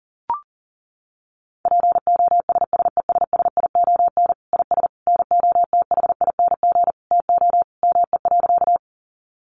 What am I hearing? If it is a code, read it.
Morse code, heard as a keyed tone, about 40 wpm, 700 Hz. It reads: POSSESSION IS NOTHING TO ME.
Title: Sentences from Top 1400 Words 40wpm